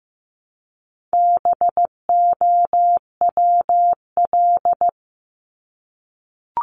Morse code, heard as a keyed tone, about 15 wpm, 700 Hz. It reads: BOWL
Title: Top 1001-1250 Words 15wpm